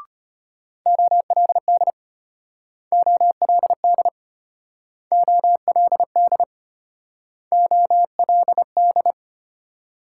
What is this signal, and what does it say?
Morse code, heard as a keyed tone, about 32 wpm, 700 Hz. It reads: OLD OLD OLD OLD